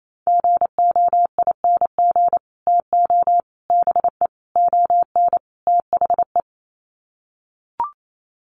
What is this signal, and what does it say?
Morse code, heard as a keyed tone, about 28 wpm, 700 Hz. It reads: GOING TO BE ON THE
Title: Sets of 5 Words Encore2 28wpm